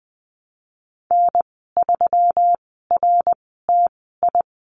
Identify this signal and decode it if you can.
Morse code, heard as a keyed tone, about 20 wpm, spaced as for 14 wpm, 700 Hz. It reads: N3RTI
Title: 122 Lesson 20g - Number 3 - Call Signs 20wpm